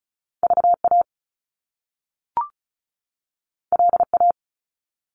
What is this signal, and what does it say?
Morse code, heard as a keyed tone, about 36 wpm, 700 Hz. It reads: VA LA
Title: US States 35wpm